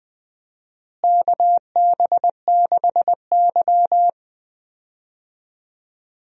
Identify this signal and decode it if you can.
Morse code, heard as a keyed tone, about 20 wpm, 700 Hz. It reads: KB6Y
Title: USA Call Signs - 4 Characters - Repeated 3x - 20wpm